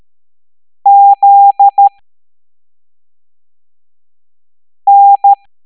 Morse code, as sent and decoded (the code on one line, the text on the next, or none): --.. -.
ZN